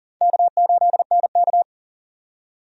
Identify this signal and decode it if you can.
Morse code, heard as a keyed tone, about 40 wpm, 700 Hz. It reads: K8NK
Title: USA Call Signs - 4 Characters - 40wpm